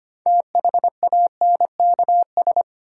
Morse code, heard as a keyed tone, about 25 wpm, 700 Hz. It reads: THANKS
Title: Sets of 5 Words - New Word Drill - 25wpm